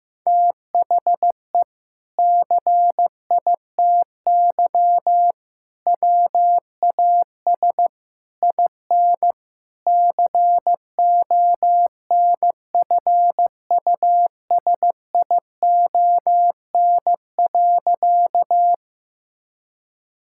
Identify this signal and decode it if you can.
Morse code, heard as a keyed tone, about 16 wpm, 700 Hz. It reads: THE CITY WAS IN CONFUSION.
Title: Sentences from Top 1600 Words 15wpm